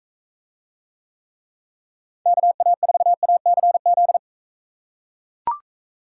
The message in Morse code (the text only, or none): KA4ACZ